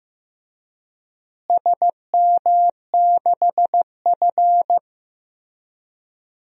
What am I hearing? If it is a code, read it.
Morse code, heard as a keyed tone, about 15 wpm, 700 Hz. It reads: SM6F
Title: DXCC-100 Call Signs - 4 Characters - Repeated 3x - 15wpm